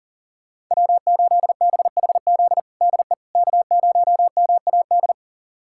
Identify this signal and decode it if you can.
Morse code, heard as a keyed tone, about 40 wpm, 700 Hz. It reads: W8BHZ DE K0MAD
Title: USA Callsign DE Callsign - 5 Characters - 40wpm